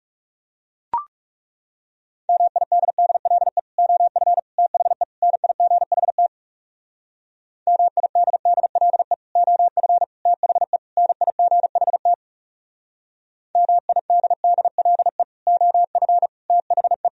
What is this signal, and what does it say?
Morse code, heard as a keyed tone, about 38 wpm, 700 Hz. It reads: MIDDLE OF THE NIGHT MIDDLE OF THE NIGHT MIDDLE OF THE